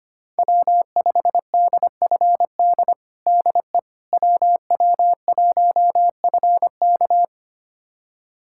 Morse code, heard as a keyed tone, about 25 wpm, 700 Hz. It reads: W5DFD DE WW1FK
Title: USA Callsign DE Callsign - 5 Characters - 25wpm